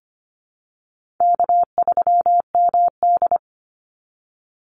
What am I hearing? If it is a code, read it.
Morse code, heard as a keyed tone, about 25 wpm, 700 Hz. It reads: K3MD
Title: Easy Call Signs - 25wpm